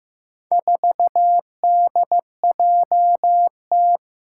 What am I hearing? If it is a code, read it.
Morse code, heard as a keyed tone, about 15 wpm, 700 Hz. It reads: 4DJT
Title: Four Challenging Letter-Number Combinations 15wpm